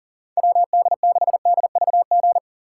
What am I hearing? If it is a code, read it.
Morse code, heard as a keyed tone, about 40 wpm, 700 Hz. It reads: WD6DVG
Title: USA Call Signs - 6 Characters - 6x Character Spacing - 40wpm